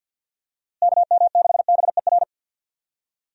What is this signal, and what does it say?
Morse code, heard as a keyed tone, about 46 wpm, 700 Hz. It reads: KM6BER
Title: USA Call Signs - 6 Characters - Repeated 3x - 50wpm